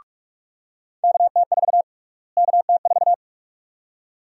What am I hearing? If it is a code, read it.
Morse code, heard as a keyed tone, about 43 wpm, 700 Hz. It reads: KT4 KT4
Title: USA Call Sign Prefixes - Repeated 2x - 45wpm